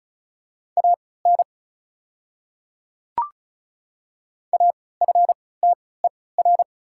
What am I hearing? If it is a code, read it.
Morse code, heard as a keyed tone, about 36 wpm, spaced as for 20 wpm, 700 Hz. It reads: AN AFTER
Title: 3x Character Spacing - Top 300 Words - 35wpm